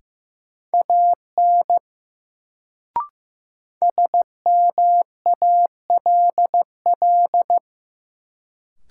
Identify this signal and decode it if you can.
Morse code, heard as a keyed tone, about 16 wpm, 700 Hz. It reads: AN SMALL
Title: Sets of 2 Words - New Word Drill - 15wpm